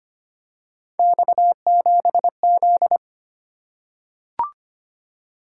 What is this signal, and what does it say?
Morse code, heard as a keyed tone, about 25 wpm, 700 Hz. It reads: X7Z
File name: Three Letter-Number Combinations - 25wpm